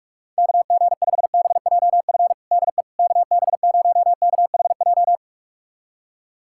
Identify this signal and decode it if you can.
Morse code, heard as a keyed tone, about 41 wpm, 700 Hz. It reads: KG5BJF DE KB0KHJ
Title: USA Callsign DE Callsign - 6 Characters - 45wpm